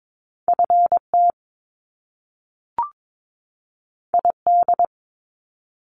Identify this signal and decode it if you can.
Morse code, heard as a keyed tone, about 22 wpm, 700 Hz. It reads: FT ID
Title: Top Two Letter Combinations in Words 22wpm